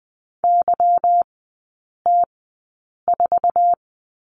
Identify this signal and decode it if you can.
Morse code, heard as a keyed tone, about 20 wpm, spaced as for 8 wpm, 700 Hz. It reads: YT4
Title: 159 Lesson 26e - Letter g - Three Characters 20wpm